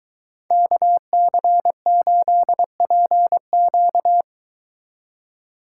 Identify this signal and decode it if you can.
Morse code, heard as a keyed tone, about 22 wpm, 700 Hz. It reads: KC8PQ